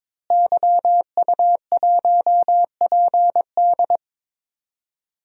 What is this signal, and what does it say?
Morse code, heard as a keyed tone, about 21 wpm, 700 Hz. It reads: YU1PD
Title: DXCC-100 Call Signs - 5 Characters - Repeated 2x - 22wpm